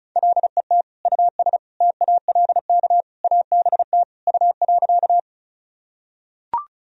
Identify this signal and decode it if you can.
Morse code, heard as a keyed tone, about 36 wpm, 700 Hz. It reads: LET US TALK ABT U.
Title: Sentences from Top 400 Words and Abbreviations 35wpm